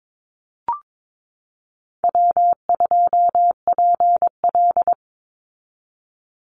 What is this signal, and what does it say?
Morse code, heard as a keyed tone, about 22 wpm, 700 Hz. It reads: W2PL
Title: Easy Call Signs Repeated 3x - 22wpm